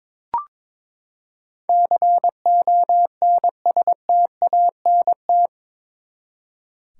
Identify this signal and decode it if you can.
Morse code, heard as a keyed tone, about 22 wpm, 700 Hz. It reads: CONSTANT
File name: practice / 22wpm / Eight Letter Words - 22wpm